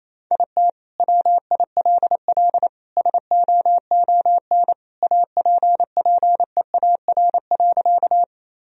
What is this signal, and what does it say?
Morse code, heard as a keyed tone, about 28 wpm, 700 Hz. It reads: IT WILL SOON APPEAR.
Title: Sentences from Top 1000 Words - Review Part 2 - 28wpm